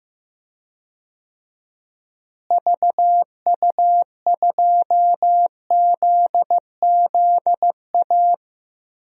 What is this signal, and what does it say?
Morse code, heard as a keyed tone, about 15 wpm, 700 Hz. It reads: VU2ZZA
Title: DXCC-100 Call Signs - 6 Characters - Repeated 3x - 15wpm